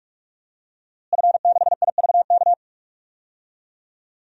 Morse code, heard as a keyed tone, about 46 wpm, 700 Hz. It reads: F6IVK
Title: DXCC-100 Call Signs - 5 Characters - Repeated 3x - 45wpm